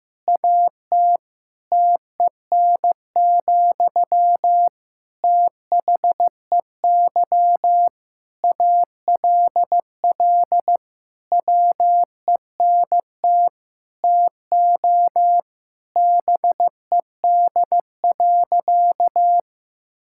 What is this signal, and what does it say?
Morse code, heard as a keyed tone, about 15 wpm, 700 Hz. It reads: AT TEN, THEY ALL WENT TO BED.
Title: Sentences from Top 1000 Words - Review Part 6 - 15wpm